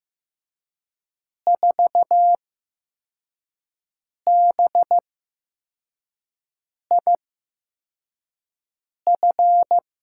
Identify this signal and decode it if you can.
Morse code, heard as a keyed tone, about 15 wpm, spaced as for 4 wpm, 700 Hz. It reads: 4BIF
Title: USA Call Signs - 6 Characters - 8x Character Spacing - 15wpm